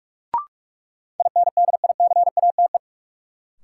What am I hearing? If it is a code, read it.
Morse code, heard as a keyed tone, about 46 wpm, 700 Hz. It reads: INDICATE
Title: Sentences from Top 1300 Words - New Word Drill - 45wpm